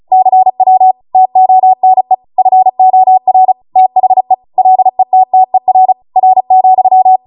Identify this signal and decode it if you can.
Morse code, heard as a keyed tone, about 36 wpm, 750 Hz. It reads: CW TONE FOR THE LETTER R,